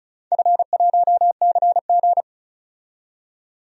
Morse code, heard as a keyed tone, about 34 wpm, 700 Hz. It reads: F1CG
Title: DXCC-100 Call Signs - 4 Characters - 35wpm